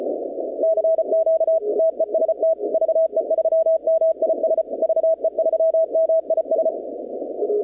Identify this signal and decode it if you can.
Morse code, heard as a keyed tone, about 33 wpm, 650 Hz. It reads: CQ TEST VE3MIS VE3MIS
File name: contests / 2019-cw-wpx-kt5j-k5pi- -2330 0-21